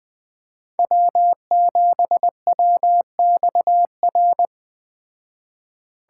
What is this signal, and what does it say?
Morse code, heard as a keyed tone, about 20 wpm, 700 Hz. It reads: W7WXR